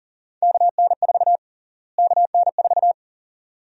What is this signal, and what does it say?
Morse code, heard as a keyed tone, about 40 wpm, 700 Hz. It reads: KN4 KN4
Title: USA Call Sign Prefixes - Repeated 2x - 40wpm